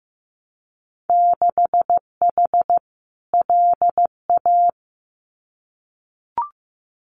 Morse code, heard as a keyed tone, about 15 wpm, 700 Hz. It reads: 6H LA
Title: Winter Field Day Exchanges Encore 15wpm